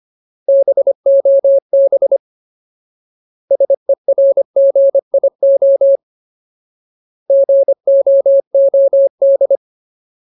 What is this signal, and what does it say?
Morse code, heard as a keyed tone, about 25 wpm, 550 Hz. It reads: BOB SERGIO GOOD
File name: Ultimate Word List / 25WPM Ultimate